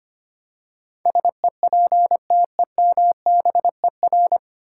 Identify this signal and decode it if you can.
Morse code, heard as a keyed tone, about 25 wpm, 700 Hz. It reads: SEPTEMBER
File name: Nine Letter Words - 25wpm